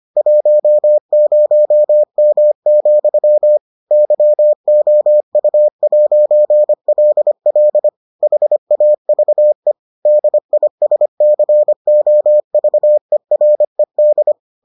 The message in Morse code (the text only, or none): 10M, YOU'LL HAVE DISCOVERED